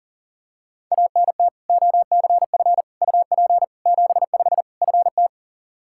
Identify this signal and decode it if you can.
Morse code, heard as a keyed tone, about 40 wpm, 700 Hz. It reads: ANT OCF UP 75 FT
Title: Practice QSO - Exchange 2 - Info - 40wpm